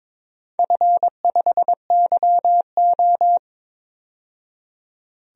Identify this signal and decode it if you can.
Morse code, heard as a keyed tone, about 22 wpm, 700 Hz. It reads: F5YO
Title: DXCC-100 Call Signs - 4 Characters - 22wpm